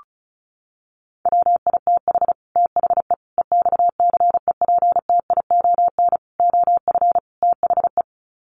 Morse code, heard as a keyed tone, about 34 wpm, 700 Hz. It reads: WITH THE EXCEPTION OF THE
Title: Sets of 5 Words Encore 35wpm